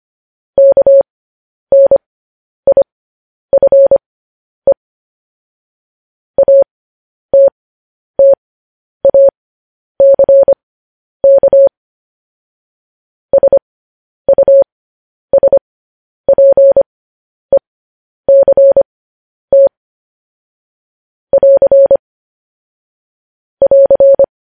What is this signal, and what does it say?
Morse code, heard as a keyed tone, about 25 wpm, spaced as for 10 wpm, 550 Hz. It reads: KNIFE ATTACK SUSPECT <AR> <AR>
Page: news-headlines-in-morse-code-at-10-wpm